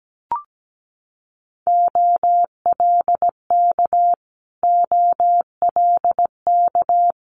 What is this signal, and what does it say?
Morse code, heard as a keyed tone, about 17 wpm, 690 Hz. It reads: OLK OLK
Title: Multi-letter Phonemes Repeated 3x 17wpm